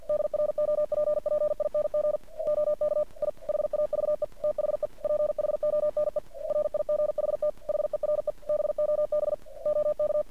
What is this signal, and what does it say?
Morse code, heard as a keyed tone, about 50 wpm, 600 Hz. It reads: DROPPING OK I HAVE THE PHONE RIGHT HERE BOB OK
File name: archived / CRSnet